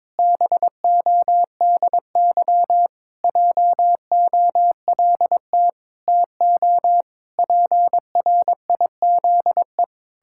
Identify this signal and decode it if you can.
Morse code, heard as a keyed tone, about 22 wpm, 700 Hz. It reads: BODY JOLT TO PRIZE
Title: Pangram Challenge v2 - 22wpm